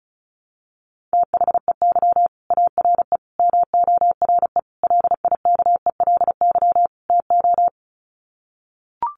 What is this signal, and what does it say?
Morse code, heard as a keyed tone, about 34 wpm, 700 Hz. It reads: THEY ARE MORE LIKELY TO
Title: Sets of 5 Words Encore2 35wpm